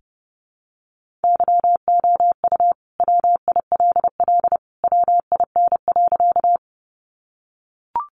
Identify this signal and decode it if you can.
Morse code, heard as a keyed tone, about 30 wpm, 700 Hz. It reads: YOU WILL WIN.
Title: Sentences from Top 1500 Words 30wpm